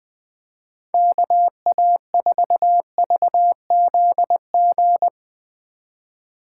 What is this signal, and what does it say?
Morse code, heard as a keyed tone, about 23 wpm, 700 Hz. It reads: KA4VZG
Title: Speed Racing - USA Call Signs 20wpm